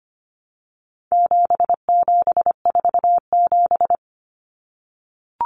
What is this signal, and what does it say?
Morse code, heard as a keyed tone, about 25 wpm, 700 Hz. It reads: 7747
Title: Common Cut Numbers - 4-Digit 25wpm